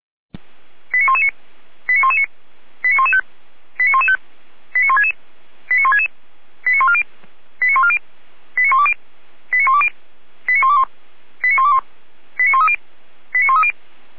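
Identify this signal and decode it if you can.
Five-tone selective call (ZVEI), heard as quick runs of five tones, tones of about 80 ms. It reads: 89119 89119 89115 89115 89166 89166 89140 89140 89120 89120 89121 89121 89130 89130
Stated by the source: Five-tone ZVEI selective call received on an SDR.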